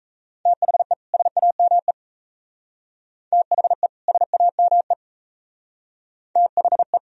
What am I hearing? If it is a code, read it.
Morse code, heard as a keyed tone, about 34 wpm, 700 Hz. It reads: THE SAME THE SAME THE
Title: Speed-Racing - Sets of 2 Words 28wpm